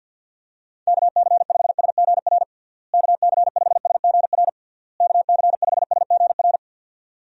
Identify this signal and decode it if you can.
Morse code, heard as a keyed tone, about 51 wpm, 700 Hz. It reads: KC5SGR KC5SGR KC5SGR
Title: USA Call Signs - 6 Characters - Repeated 3x - 50wpm